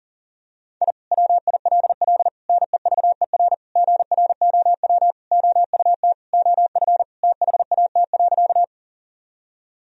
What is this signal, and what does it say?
Morse code, heard as a keyed tone, about 40 wpm, 700 Hz. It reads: I WILL NEVER GROW OUT OF THAT.
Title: Sentences from Top 1000 Words - Review Part 1 - 40wpm